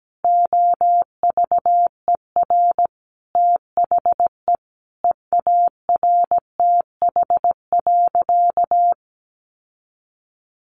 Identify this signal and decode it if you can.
Morse code, heard as a keyed tone, about 17 wpm, 700 Hz. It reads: OVER THE EARTH.
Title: Sentences from Top 1200 Words 17wpm